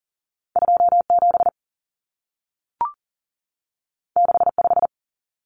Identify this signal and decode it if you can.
Morse code, heard as a keyed tone, about 40 wpm, 700 Hz. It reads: 27 65
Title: Numbers - 2-Digit 40wpm